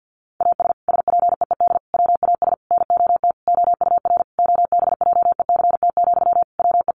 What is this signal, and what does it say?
Morse code, heard as a keyed tone, about 50 wpm, 700 Hz. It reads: AS SPEED WAS NOT OUR OBJECT, WE
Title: Sentences from Top 800 Words 50wpm